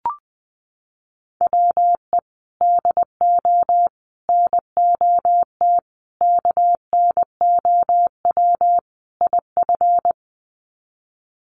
Morse code, heard as a keyed tone, about 20 wpm, 700 Hz. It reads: WE DO NOT KNOW IF
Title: Sets of 5 Words Encore2 20wpm